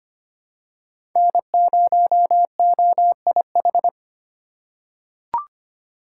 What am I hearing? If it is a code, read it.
Morse code, heard as a keyed tone, about 25 wpm, 700 Hz. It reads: N0OIH
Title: USA Call Signs - 5 Characters - 3x Character Spacing - 25wpm